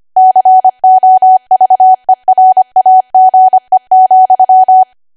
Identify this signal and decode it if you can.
Morse code, heard as a keyed tone, about 25 wpm, 750 Hz. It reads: COVERAGE,
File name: Archive / 25 WPM / 25WPM